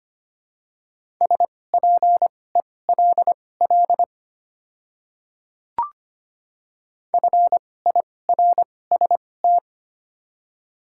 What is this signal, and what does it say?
Morse code, heard as a keyed tone, about 25 wpm, spaced as for 18 wpm, 700 Hz. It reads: SPELL FIRST
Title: 2x Character Spacing - Top 300 Words - 25wpm